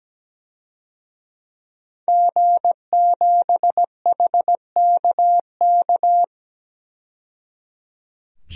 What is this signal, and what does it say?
Morse code, heard as a keyed tone, about 17 wpm, 700 Hz. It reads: G7HKK
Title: DXCC-100 Call Signs - 5 Characters - 17wpm